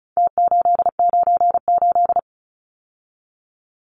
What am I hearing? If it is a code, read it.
Morse code, heard as a keyed tone, about 34 wpm, 700 Hz. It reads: T898
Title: Numbers - Contest Style - 4-Digit 35wpm